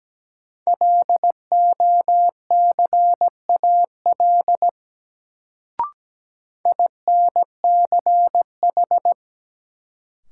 Morse code, heard as a keyed tone, about 17 wpm, 700 Hz. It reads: LOCAL INCH
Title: Sentences from Top 1000 Words - Review Part 3 - New Word Drill - 17wpm